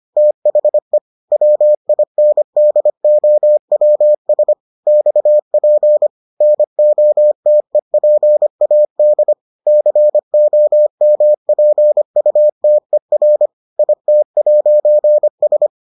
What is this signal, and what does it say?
Morse code, heard as a keyed tone, about 25 wpm, 600 Hz. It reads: THE WINDOWS XP NOTEPAD COMPUTER IT'S